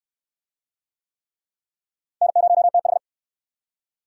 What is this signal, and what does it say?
Morse code, heard as a keyed tone, about 62 wpm, 700 Hz. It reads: N0TH